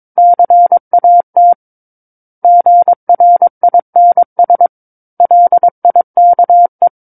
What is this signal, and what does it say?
Morse code, heard as a keyed tone, about 22 wpm, 700 Hz. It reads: CAT GRINS LIKE
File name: 06-Chapter-06 22 WPM